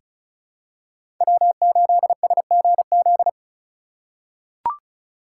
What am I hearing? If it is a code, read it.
Morse code, heard as a keyed tone, about 36 wpm, 700 Hz. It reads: W8SGZ
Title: USA Call Signs - 4x Character Spacing - 35wpm